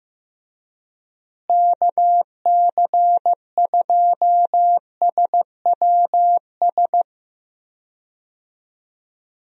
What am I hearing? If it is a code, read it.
Morse code, heard as a keyed tone, about 15 wpm, 700 Hz. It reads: KC2SWS
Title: USA Call Signs - 6 Characters - Repeated 3x - 15wpm